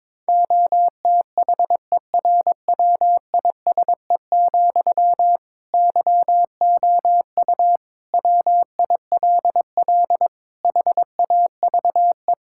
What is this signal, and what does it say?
Morse code, heard as a keyed tone, about 22 wpm, 700 Hz. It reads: OTHERWISE, YOU WILL HAVE
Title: Sentences from Top 1600 Words 22wpm